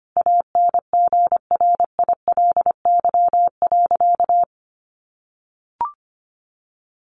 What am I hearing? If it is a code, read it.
Morse code, heard as a keyed tone, about 25 wpm, 700 Hz. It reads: ANGRILY.
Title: A Sound Of Thunder 25wpm